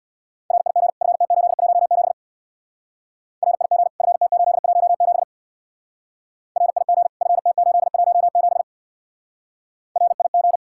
Morse code, heard as a keyed tone, about 64 wpm, 700 Hz. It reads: RIG FT817 RIG FT817 RIG FT817 RIG